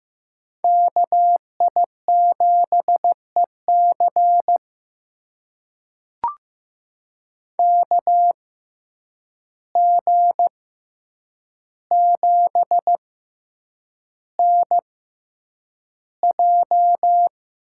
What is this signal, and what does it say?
Morse code, heard as a keyed tone, about 15 wpm, spaced as for 8 wpm, 700 Hz. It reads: KI7EC KG7NJ
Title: USA Call Signs - 6x Character Spacing - 15wpm